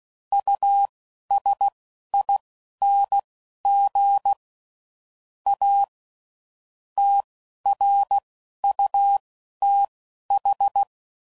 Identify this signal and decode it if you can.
Morse code, heard as a keyed tone, about 16 wpm, spaced as for 11 wpm, 800 Hz. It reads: USING A TRUTH
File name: CW Code Groups @10WPM / Technical Subjects / Gates Intro